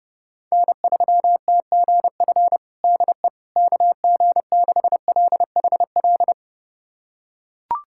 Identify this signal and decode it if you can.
Morse code, heard as a keyed tone, about 32 wpm, 700 Hz. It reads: N3TGF DE KG6LHL
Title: USA Callsign DE Callsign 30wpm